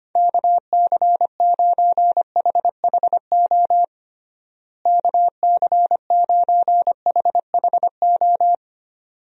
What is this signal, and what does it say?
Morse code, heard as a keyed tone, about 25 wpm, 700 Hz. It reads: KC9HHO KC9HHO
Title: USA Call Signs - 6 Characters - Repeated 2x - 25wpm